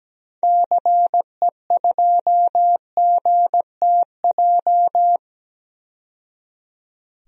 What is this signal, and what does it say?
Morse code, heard as a keyed tone, about 17 wpm, 700 Hz. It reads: CE2GTJ
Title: DXCC-100 Call Signs - 6 Characters - 17wpm